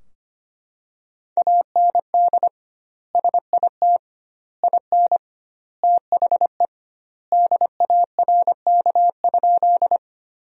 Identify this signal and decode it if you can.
Morse code, heard as a keyed tone, about 25 wpm, 700 Hz. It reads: AND SIT IN THE DARK?
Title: 2x Word-Spacing - Sentences from Top 500 Words - 25wpm